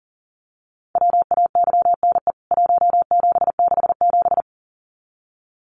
Morse code, heard as a keyed tone, about 40 wpm, 700 Hz. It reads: WAYNE 1767
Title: CWT Exchange 40wpm